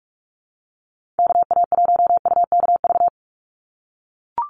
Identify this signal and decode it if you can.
Morse code, heard as a keyed tone, about 44 wpm, 690 Hz. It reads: KA1UKV
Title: USA Call Signs 45wpm